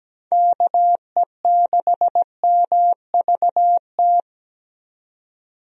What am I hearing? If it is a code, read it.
Morse code, heard as a keyed tone, about 17 wpm, 700 Hz. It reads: KE6MVT
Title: USA Call Signs - 6 Characters - 17wpm